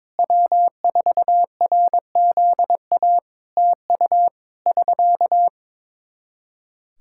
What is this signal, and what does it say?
Morse code, heard as a keyed tone, about 22 wpm, 700 Hz. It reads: W4RZA TU <SK>
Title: Practice QSO - Exchange 4 - Ending - 22wpm